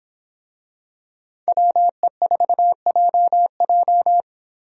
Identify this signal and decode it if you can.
Morse code, heard as a keyed tone, about 25 wpm, 700 Hz. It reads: WE4JJ